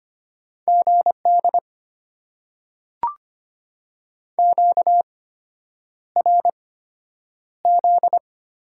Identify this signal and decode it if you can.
Morse code, heard as a keyed tone, about 25 wpm, spaced as for 9 wpm, 700 Hz. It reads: GD QRZ
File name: Top 90 QSO Elements - 8x Character Spacing - 25wpm